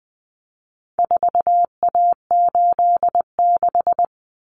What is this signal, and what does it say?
Morse code, heard as a keyed tone, about 20 wpm, 700 Hz. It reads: 4A86
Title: Common Cut Numbers - 4-Digit 20wpm